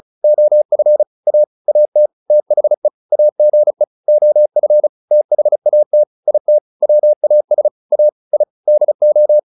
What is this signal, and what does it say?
Morse code, heard as a keyed tone, about 35 wpm, 600 Hz. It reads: OF A AT THE AGE OF THAT IT WAS A I DO